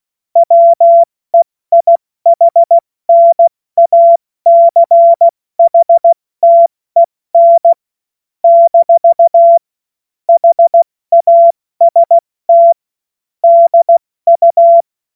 Morse code, heard as a keyed tone, about 16 wpm, 680 Hz. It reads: WEIHNACHTEN - HAST DU
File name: Morsekurs / Witze / Witz 10. Januar 2023 70 F80